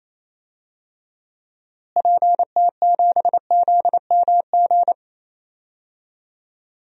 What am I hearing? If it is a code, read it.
Morse code, heard as a keyed tone, about 28 wpm, 700 Hz. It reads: PT7ZMG
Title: DXCC-100 Call Signs - 6 Characters - Repeated 2x - 28wpm